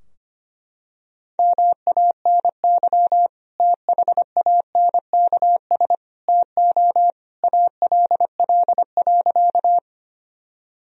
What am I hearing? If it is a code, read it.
Morse code, heard as a keyed tone, about 25 wpm, 700 Hz. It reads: MANY THANKS TO ALL.
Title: Sentences from Top 1000 Words - Review Part 3 - 25wpm